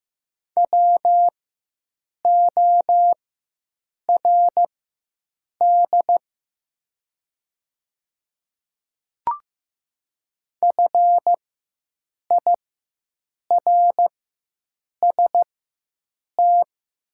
Morse code, heard as a keyed tone, about 15 wpm, spaced as for 7 wpm, 690 Hz. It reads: WORD FIRST
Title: 4x Character Spacing - Top 500 Words - 15wpm